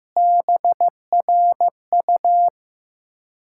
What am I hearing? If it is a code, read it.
Morse code, heard as a keyed tone, about 15 wpm, 700 Hz. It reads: BRU
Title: Ontario QSO Party Exchange 15wpm